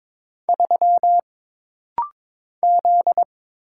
Morse code, heard as a keyed tone, about 22 wpm, 700 Hz. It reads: 3 Z
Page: This rapid-fire timing is geared to push your ICR (Instant Character Recognition) skills.